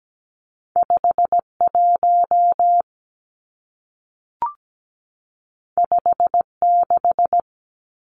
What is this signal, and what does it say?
Morse code, heard as a keyed tone, about 17 wpm, 700 Hz. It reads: 5 1 5 6
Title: Two Challenging Letter-Number Combinations 17wpm